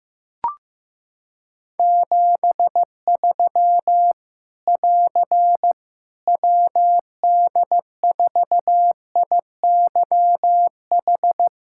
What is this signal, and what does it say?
Morse code, heard as a keyed tone, about 15 wpm, 700 Hz. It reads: 73 <AR> WD4IYH
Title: K1USN SST QSO Practice 15wpm